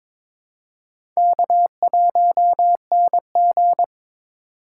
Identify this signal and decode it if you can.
Morse code, heard as a keyed tone, about 22 wpm, 700 Hz. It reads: K1NG
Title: USA Call Signs - 4 Characters - 3x Character Spacing - 22wpm